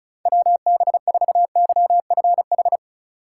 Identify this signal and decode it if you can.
Morse code, heard as a keyed tone, about 36 wpm, 700 Hz. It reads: WB4YFH
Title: USA Call Signs - 6 Characters - 4x Character Spacing - 35wpm